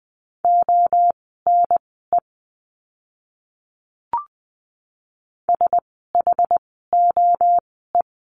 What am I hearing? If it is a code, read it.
Morse code, heard as a keyed tone, about 20 wpm, spaced as for 15 wpm, 700 Hz. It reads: ONE SHOE
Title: 068 Lesson 12f - Letter d - Words 20wpm